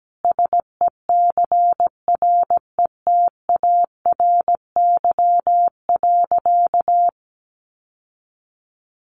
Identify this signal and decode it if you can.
Morse code, heard as a keyed tone, about 17 wpm, 700 Hz. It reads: SECRETARY.
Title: Sentences from Top 1300 Words 17wpm